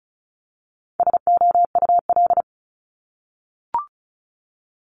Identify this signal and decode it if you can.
Morse code, heard as a keyed tone, about 36 wpm, 690 Hz. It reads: SOUL
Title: Multi-letter Phonemes 35wpm